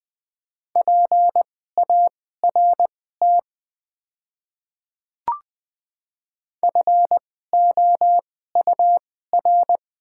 Morse code, heard as a keyed tone, about 21 wpm, spaced as for 15 wpm, 690 Hz. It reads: PART FOUR
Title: 2x Character Spacing - Top 500 Words - 20wpm